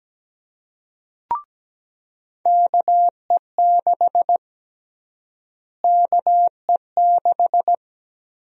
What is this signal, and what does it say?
Morse code, heard as a keyed tone, about 17 wpm, 700 Hz. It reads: KE6 KE6
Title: USA Call Sign Prefixes - Repeated 2x - 17wpm